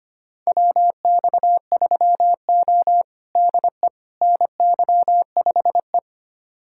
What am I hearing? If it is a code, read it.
Morse code, heard as a keyed tone, about 25 wpm, 700 Hz. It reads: WX3O DE NY5E
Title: USA Callsign DE Callsign - 4 Characters - 25wpm